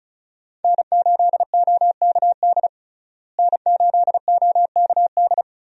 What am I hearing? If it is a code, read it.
Morse code, heard as a keyed tone, about 36 wpm, 700 Hz. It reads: N8OKD N8OKD
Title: USA Call Signs - 5 Characters - Repeated 2x - 35wpm